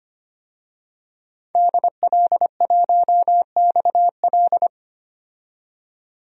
Morse code, heard as a keyed tone, about 25 wpm, 690 Hz. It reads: DL1XL
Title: DXCC-100 Call Signs - 5 Characters - 25wpm